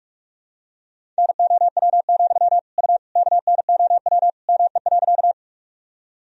Morse code, heard as a keyed tone, about 41 wpm, 700 Hz. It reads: NOW, U KNOW ME.
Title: Sentences from Top 200 Words and Abbreviations 45wpm